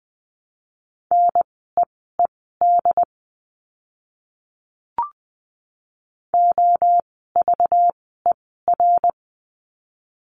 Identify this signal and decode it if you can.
Morse code, heard as a keyed tone, about 20 wpm, spaced as for 14 wpm, 700 Hz. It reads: NEED OVER
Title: 192 Lesson 31f - Letter v - Words 20wpm